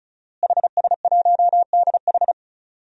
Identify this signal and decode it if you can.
Morse code, heard as a keyed tone, about 34 wpm, 700 Hz. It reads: HS1DH
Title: DXCC-100 Call Signs - 5 Characters - Repeated 2x - 35wpm